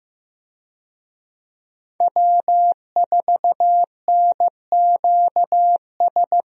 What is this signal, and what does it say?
Morse code, heard as a keyed tone, about 15 wpm, 700 Hz. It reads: W4NQS
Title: USA Call Signs - 5 Characters - Repeated 2x - 15wpm